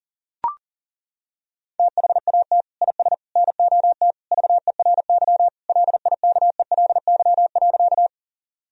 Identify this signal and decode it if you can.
Morse code, heard as a keyed tone, about 40 wpm, 700 Hz. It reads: THAT IS NOT VERY LIKELY.
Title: Sentences from Top 1000 Words - Review Part 4 - 40wpm